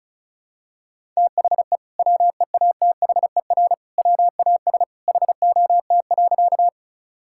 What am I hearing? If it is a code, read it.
Morse code, heard as a keyed tone, about 36 wpm, 700 Hz. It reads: THE WEATHER WAS HOT.
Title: Sentences from Top 1000 Words - Review Part 5 - 35wpm